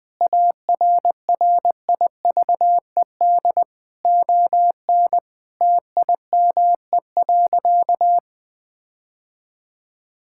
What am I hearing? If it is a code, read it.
Morse code, heard as a keyed tone, about 19 wpm, 700 Hz. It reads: ARRIVED ON TIME.
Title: Sentences with 6-Word Phrases - 20wpm